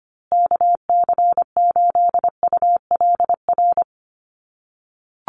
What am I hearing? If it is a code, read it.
Morse code, heard as a keyed tone, about 25 wpm, 700 Hz. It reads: KC8ULR
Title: USA Call Signs Repeated 2x 25wpm